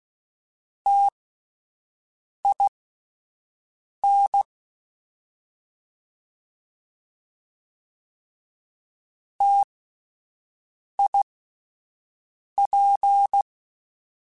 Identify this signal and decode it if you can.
Morse code, heard as a keyed tone, about 16 wpm, spaced as for 5 wpm, 800 Hz. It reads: TIN TIP